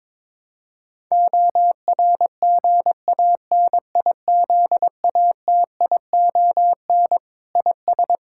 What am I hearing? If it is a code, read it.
Morse code, heard as a keyed tone, about 21 wpm, 700 Hz. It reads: ORGANIZATION IS
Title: Sentences from Top 1000 Words - Review Part 5 - 22wpm